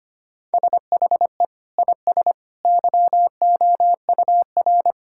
Morse code, heard as a keyed tone, about 25 wpm, 700 Hz. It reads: SHE IS YOUR
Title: Sentences from Top 1600 Words 25wpm